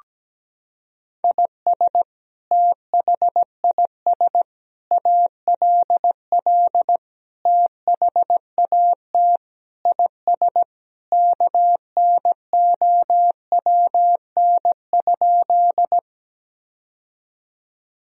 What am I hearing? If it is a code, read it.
Morse code, heard as a keyed tone, about 17 wpm, 700 Hz. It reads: IS THIS ALL THAT IS KNOWN?
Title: Sentences from Top 1000 Words - Review Part 3 - 17wpm